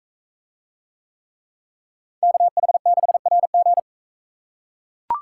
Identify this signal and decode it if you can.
Morse code, heard as a keyed tone, about 41 wpm, 700 Hz. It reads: KH6RG